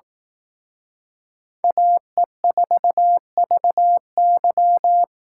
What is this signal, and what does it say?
Morse code, heard as a keyed tone, about 18 wpm, 700 Hz. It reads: AE4VY
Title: Speed Racing - USA 2x2 Call Signs 15wpm